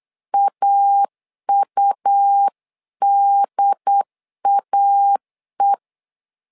Morse code, heard as a keyed tone, about 8 wpm, 800 Hz. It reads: AUDAE
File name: LEZIONI / 3 MWRO / 3 MW 8wpm 2501